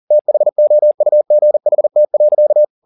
Morse code, heard as a keyed tone, about 40 wpm, 600 Hz. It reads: THOUGHT.
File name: the old man and the sea 2 40wpm